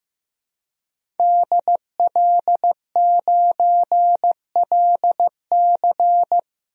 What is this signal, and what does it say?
Morse code, heard as a keyed tone, about 15 wpm, 700 Hz. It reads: DL9LC
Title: DXCC-100 Call Signs - 5 Characters - 15wpm